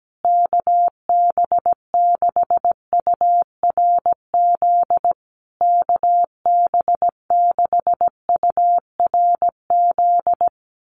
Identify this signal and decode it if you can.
Morse code, heard as a keyed tone, about 17 wpm, 700 Hz. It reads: KB6URZ KB6URZ
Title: USA Call Signs Repeated 3x 17wpm